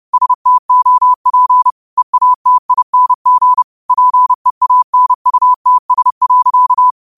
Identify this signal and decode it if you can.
Morse code, heard as a keyed tone, about 30 wpm, 1000 Hz. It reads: STOP EATING PEANUTS.